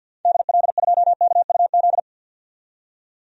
Morse code, heard as a keyed tone, about 47 wpm, 700 Hz. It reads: DL2KUZ
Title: DXCC-100 Call Signs - 6 Characters - 50wpm